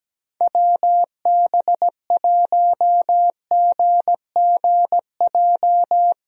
Morse code, heard as a keyed tone, about 17 wpm, 700 Hz. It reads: WB1GGJ
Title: USA Call Signs - 6 Characters - Repeated 2x - 17wpm